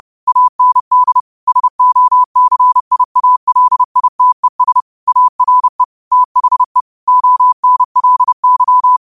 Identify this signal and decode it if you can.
Morse code, heard as a keyed tone, about 30 wpm, 1000 Hz. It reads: AND SOCIALITES ARE THE ONLY
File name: qotdfiles / 30wpm